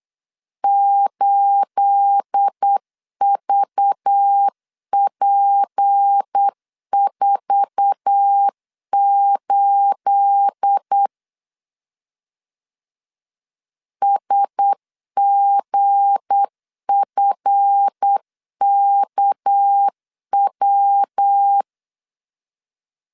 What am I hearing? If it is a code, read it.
Morse code, heard as a keyed tone, about 8 wpm, 800 Hz. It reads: 8VP48 SGFKW
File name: CW / DET / 1 9 1500C S C 8wpm 5653